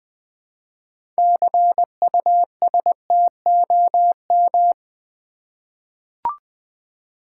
Morse code, heard as a keyed tone, about 20 wpm, 700 Hz. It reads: CUSTOM
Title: 6 Letter Words - First 3 Letters Given - 20wpm